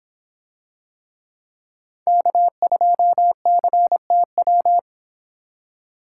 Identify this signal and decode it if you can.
Morse code, heard as a keyed tone, about 24 wpm, 700 Hz. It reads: K2CTW